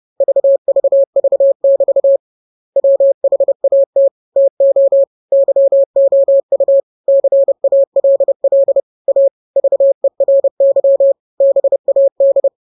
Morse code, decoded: VVV<BT> WHAT TO YOU CALL A VERY BAD